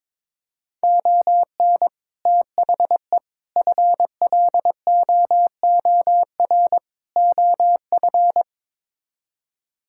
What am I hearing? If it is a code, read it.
Morse code, heard as a keyed tone, about 21 wpm, 700 Hz. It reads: ON THE FLOOR OF
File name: practice / 22wpm / Build A Phrase - 5-Word Phrases - 22wpm